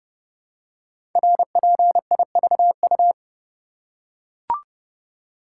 Morse code, heard as a keyed tone, about 30 wpm, 700 Hz. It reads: RPIVU
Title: Code Groups - 5 Letters - 30wpm